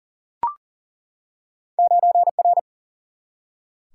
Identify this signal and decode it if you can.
Morse code, heard as a keyed tone, about 38 wpm, 700 Hz. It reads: 9R
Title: Single-Letter Single-Number 40wpm